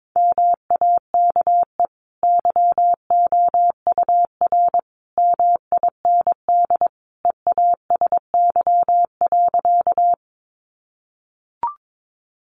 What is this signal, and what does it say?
Morse code, heard as a keyed tone, about 22 wpm, 700 Hz. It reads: MAKE YOUR MIND EASY.
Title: Sentences from Top 700 Words 22wpm